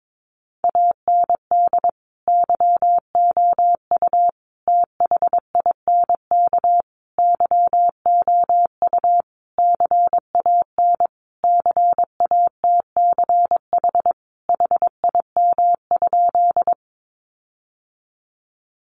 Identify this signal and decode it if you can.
Morse code, heard as a keyed tone, about 22 wpm, 700 Hz. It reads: AND YOU THINK YOU CAN CATCH HIM?
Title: Sentences from Top 1500 Words 22wpm